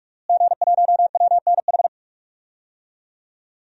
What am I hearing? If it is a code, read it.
Morse code, heard as a keyed tone, about 46 wpm, 700 Hz. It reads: G1WNH
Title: DXCC-100 Call Signs - 5 Characters - Repeated 3x - 45wpm